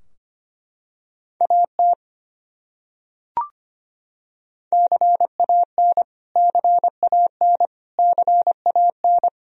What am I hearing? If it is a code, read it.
Morse code, heard as a keyed tone, about 25 wpm, 700 Hz. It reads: AT CAN CAN CAN
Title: Top 100 Words Repeated 25wpm